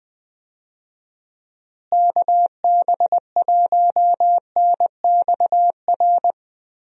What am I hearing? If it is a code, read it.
Morse code, heard as a keyed tone, about 21 wpm, 700 Hz. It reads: KB1NXR
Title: USA Call Signs - 6 Characters - Repeated 3x - 20wpm